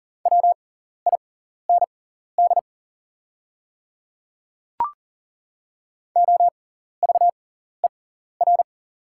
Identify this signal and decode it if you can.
Morse code, heard as a keyed tone, about 39 wpm, spaced as for 14 wpm, 690 Hz. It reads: WIND OVER
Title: 6x Character Spacing - Top 500 Words - 40wpm